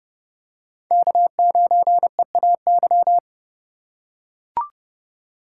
Morse code, heard as a keyed tone, about 32 wpm, 700 Hz. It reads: K9EAY